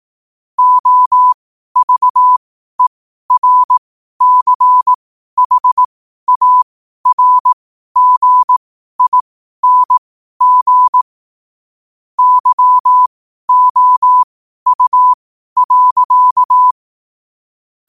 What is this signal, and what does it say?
Morse code, heard as a keyed tone, about 18 wpm, spaced as for 12 wpm, 1000 Hz. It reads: OVERCHARGING YOU.
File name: qotdfiles / 12wpm